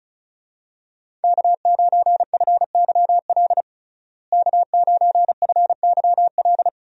Morse code, heard as a keyed tone, about 36 wpm, 700 Hz. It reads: K9FYL K9FYL
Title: USA Call Signs - 5 Characters - Repeated 2x - 35wpm